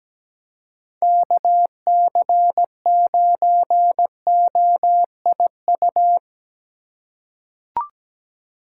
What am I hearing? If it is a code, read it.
Morse code, heard as a keyed tone, about 17 wpm, 700 Hz. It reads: KC9OIU
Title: USA Call Signs - 6 Characters - 3x Character Spacing - 17wpm